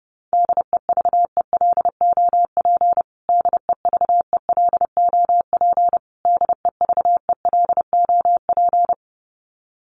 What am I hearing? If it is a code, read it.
Morse code, heard as a keyed tone, about 30 wpm, 700 Hz. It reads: DEVELOP DEVELOP DEVELOP
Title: Top 500 Words Repeated 30wpm